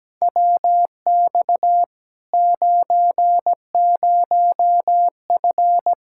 Morse code, heard as a keyed tone, about 17 wpm, 700 Hz. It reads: WX 90F
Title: Practice QSO - Exchange 2 - Info - 17wpm